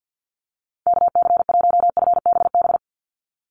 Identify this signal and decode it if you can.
Morse code, heard as a keyed tone, about 50 wpm, 700 Hz. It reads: KC1FBB
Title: USA Call Signs 50wpm